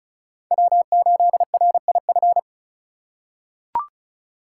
Morse code, heard as a keyed tone, about 34 wpm, 700 Hz. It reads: W8RIF
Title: USA Call Signs - 5 Characters - 3x Character Spacing - 35wpm